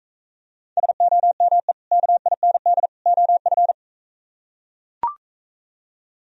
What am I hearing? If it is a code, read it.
Morse code, heard as a keyed tone, about 43 wpm, 700 Hz. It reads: SOME KIND OF